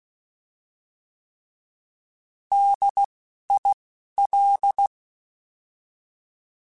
Morse code, heard as a keyed tone, about 16 wpm, spaced as for 12 wpm, 800 Hz. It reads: DIL
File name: Alphabetical / Alphabetical mp3s / Ds @10